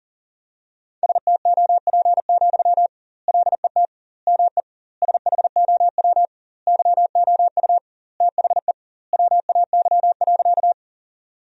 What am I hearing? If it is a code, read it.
Morse code, heard as a keyed tone, about 40 wpm, 700 Hz. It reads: STOP, LET ME SHOW YOU THE WAY.
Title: 2x Word-Spacing - Sentences from Top 300 Words - 40wpm